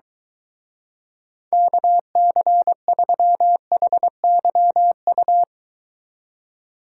Morse code, heard as a keyed tone, about 23 wpm, 700 Hz. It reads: KC3HYU